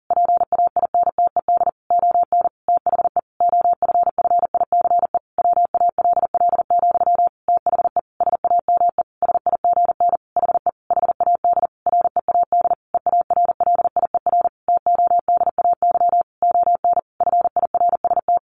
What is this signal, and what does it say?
Morse code, heard as a keyed tone, about 40 wpm, 700 Hz. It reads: PAINTED ON THE OFFICE WALL, THE SAME SIGN HE HAD READ EARLIER TODAY ON FIRST
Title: A Sound Of Thunder 40wpm